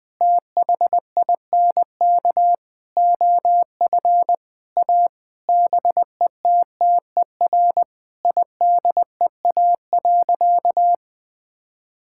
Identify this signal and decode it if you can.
Morse code, heard as a keyed tone, about 20 wpm, 700 Hz. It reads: THINK OF A BETTER IDEA.
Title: Sentences with 5-Word Phrases - 20wpm